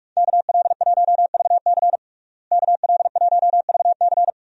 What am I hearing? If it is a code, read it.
Morse code, heard as a keyed tone, about 46 wpm, 700 Hz. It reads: KL1VC KL1VC
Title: USA Call Signs - 5 Characters - Repeated 2x - 45wpm